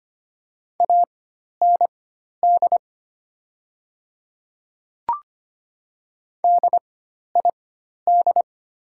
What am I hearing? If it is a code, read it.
Morse code, heard as a keyed tone, about 25 wpm, spaced as for 12 wpm, 700 Hz. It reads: AND DID
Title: 4x Character Spacing - Top 100 Words - 25wpm